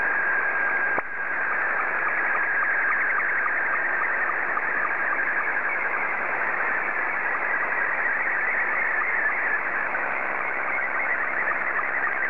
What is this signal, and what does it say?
SSTV (slow-scan television), Scottie 2.